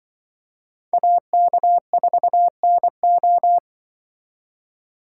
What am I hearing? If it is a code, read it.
Morse code, heard as a keyed tone, about 24 wpm, 700 Hz. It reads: AK4NO
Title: Speed Racing - USA 2x2 Call Signs 20wpm